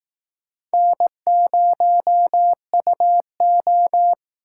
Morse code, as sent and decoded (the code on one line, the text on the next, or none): -. ----- ..- ---
N0UO